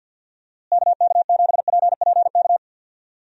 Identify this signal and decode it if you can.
Morse code, heard as a keyed tone, about 45 wpm, 700 Hz. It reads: KK7PPK
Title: USA Call Signs - 6 Characters - Repeated 2x - 50wpm